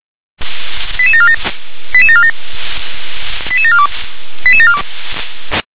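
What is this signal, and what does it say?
Five-tone selective call (ZVEI), heard as quick runs of five tones, tones of about 70 ms. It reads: 88747 88747 88742 88742
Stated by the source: ZVEI five-tone selective call received with an SDR.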